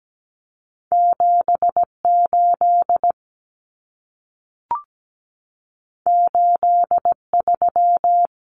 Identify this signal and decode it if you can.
Morse code, heard as a keyed tone, about 17 wpm, 700 Hz. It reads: 78 83
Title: Numbers - 2-Digit 17wpm